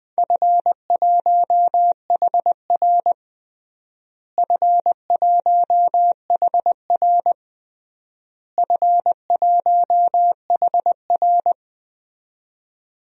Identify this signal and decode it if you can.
Morse code, heard as a keyed tone, about 21 wpm, 700 Hz. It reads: F1HR F1HR F1HR
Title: DXCC-100 Call Signs - 4 Characters - Repeated 3x - 20wpm